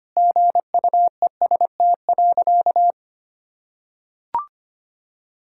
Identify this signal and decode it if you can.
Morse code, heard as a keyed tone, about 25 wpm, 700 Hz. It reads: GUEST.
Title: Sentences from Top 1600 Words 25wpm